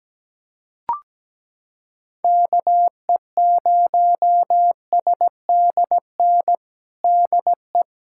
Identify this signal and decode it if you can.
Morse code, heard as a keyed tone, about 17 wpm, 700 Hz. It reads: KE0SDN DE
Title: USA Callsign DE Callsign - 6 Characters - 17wpm